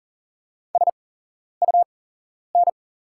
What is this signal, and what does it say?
Morse code, heard as a keyed tone, about 39 wpm, spaced as for 11 wpm, 690 Hz. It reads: SUN
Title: 8x Character Spacing - Top 500 Words - 40wpm